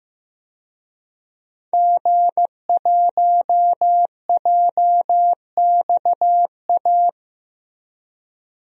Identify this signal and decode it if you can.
Morse code, heard as a keyed tone, about 15 wpm, 700 Hz. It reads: G1JXA
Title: DXCC-100 Call Signs - 5 Characters - Repeated 2x - 15wpm